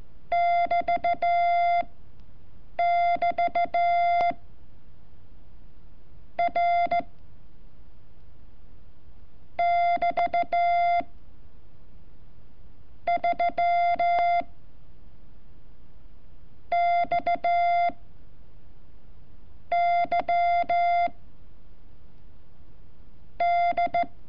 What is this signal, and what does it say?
Morse code, heard as a keyed tone, about 14 wpm, spaced as for 7 wpm, 690 Hz. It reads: <BT><BT> R <BT> 3 X Y D
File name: curso / ejercicio 29